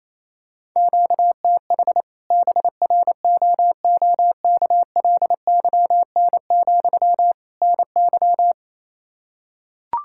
Morse code, heard as a keyed tone, about 28 wpm, 700 Hz. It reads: QTH BROOKLYN, NY
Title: Practice QSO - Exchange 1 - Beginning - 28wpm